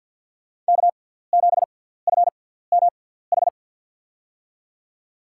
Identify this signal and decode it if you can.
Morse code, heard as a keyed tone, about 50 wpm, spaced as for 17 wpm, 700 Hz. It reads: K7FMH
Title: USA Call Signs - 5 Characters - 6x Character Spacing - 50wpm